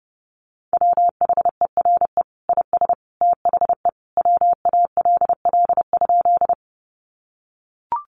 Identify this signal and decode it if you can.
Morse code, heard as a keyed tone, about 30 wpm, 700 Hz. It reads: WHERE IS THE WALL?
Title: Sentences from Top 1100 Words 30wpm